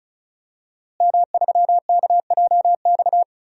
Morse code, heard as a keyed tone, about 36 wpm, 700 Hz. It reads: M3KJX
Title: DXCC-100 Call Signs - 5 Characters - Repeated 3x - 35wpm